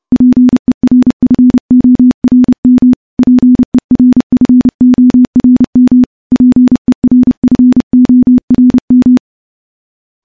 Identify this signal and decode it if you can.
Morse code, heard as a keyed tone, about 30 wpm, 250 Hz. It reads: PERFORM PERFORM PERFORM
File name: morse / morse-training-30WPM